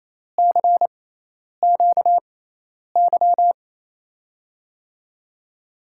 Morse code, heard as a keyed tone, about 28 wpm, spaced as for 10 wpm, 700 Hz. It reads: CQY
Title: USA Call Signs - 6 Characters - 6x Character Spacing - 28wpm